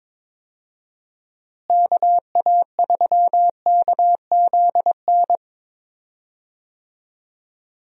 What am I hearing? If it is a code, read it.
Morse code, heard as a keyed tone, about 22 wpm, 700 Hz. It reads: KA3KZN
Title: USA Call Signs - 6 Characters - Repeated 3x - 22wpm